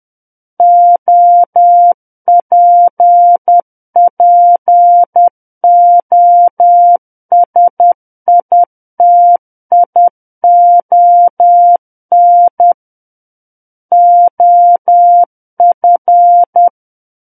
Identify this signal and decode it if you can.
Morse code, heard as a keyed tone, about 10 wpm, 700 Hz. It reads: OPPOSITION OF
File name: Book One / 01-Chapter 1 - 10 WPM